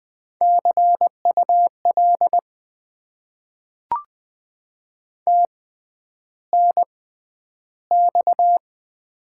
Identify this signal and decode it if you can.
Morse code, heard as a keyed tone, about 20 wpm, spaced as for 10 wpm, 700 Hz. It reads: CUL TNX
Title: Top 90 QSO Elements - 6x Character Spacing - 20wpm